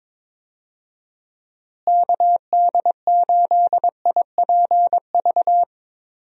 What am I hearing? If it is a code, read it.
Morse code, heard as a keyed tone, about 22 wpm, 700 Hz. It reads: KD8IPV
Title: USA Call Signs - 6 Characters - Repeated 2x - 22wpm